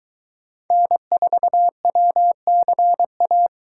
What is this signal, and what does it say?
Morse code, heard as a keyed tone, about 23 wpm, 700 Hz. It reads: N4WCA